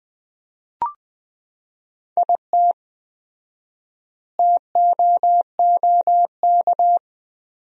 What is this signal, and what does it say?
Morse code, heard as a keyed tone, about 19 wpm, 700 Hz. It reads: IT TOOK
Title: 4x Word-Spacing - Sentences from Top 300 Words - 20wpm